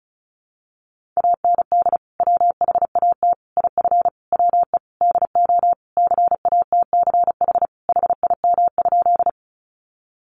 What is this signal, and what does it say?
Morse code, heard as a keyed tone, about 34 wpm, 700 Hz. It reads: AND WHAT IF WE DO CATCH HIM?
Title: Sentences from Top 1500 Words 35wpm